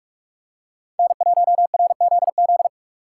Morse code, heard as a keyed tone, about 44 wpm, 700 Hz. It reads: N1RZZ